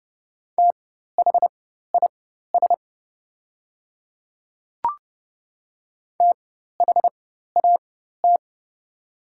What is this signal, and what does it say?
Morse code, heard as a keyed tone, about 30 wpm, spaced as for 14 wpm, 700 Hz. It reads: THIS THAT
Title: 4x Character Spacing - Top 100 Words - 30wpm